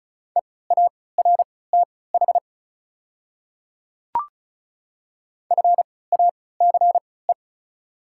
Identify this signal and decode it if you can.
Morse code, heard as a keyed tone, about 35 wpm, spaced as for 20 wpm, 690 Hz. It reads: EARTH FACE
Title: 3x Character Spacing - Top 300 Words - 35wpm